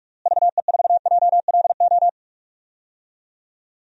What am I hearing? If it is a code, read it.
Morse code, heard as a keyed tone, about 42 wpm, 700 Hz. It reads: VE4JLO
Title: DXCC-100 Call Signs - 6 Characters - Repeated 2x - 45wpm